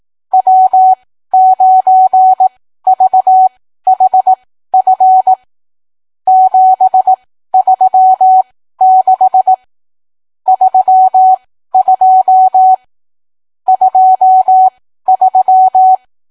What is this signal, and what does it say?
Morse code, heard as a keyed tone, about 18 wpm, spaced as for 13 wpm, 750 Hz. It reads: W9VHF 736 32 23